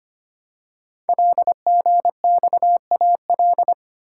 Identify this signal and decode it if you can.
Morse code, heard as a keyed tone, about 25 wpm, 700 Hz. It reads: LGXAL
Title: Code Groups - 5 Letters - 25wpm